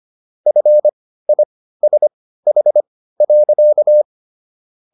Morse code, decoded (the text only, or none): FISH.